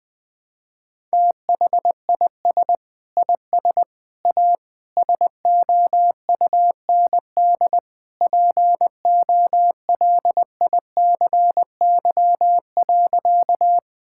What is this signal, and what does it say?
Morse code, heard as a keyed tone, about 21 wpm, 700 Hz. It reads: THIS IS A SOUND POLICY.
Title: Sentences from Top 1000 Words - Review Part 5 - 20wpm